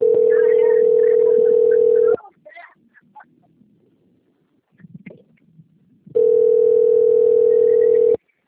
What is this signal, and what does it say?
Ringback tone (North America), 440 + 480 Hz.